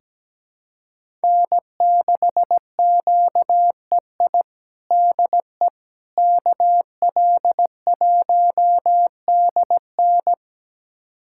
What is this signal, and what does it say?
Morse code, heard as a keyed tone, about 17 wpm, 700 Hz. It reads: N6QEI DE KL1DN
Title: USA Callsign DE Callsign - 5 Characters - 17wpm